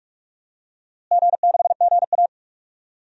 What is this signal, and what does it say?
Morse code, heard as a keyed tone, about 44 wpm, 700 Hz. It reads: G6GA
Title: DXCC-100 Call Signs - 4 Characters - Repeated 2x - 45wpm